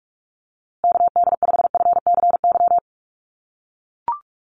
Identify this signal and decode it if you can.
Morse code, heard as a keyed tone, about 44 wpm, 700 Hz. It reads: KD5FCY
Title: USA Call Signs Repeated 2x 45wpm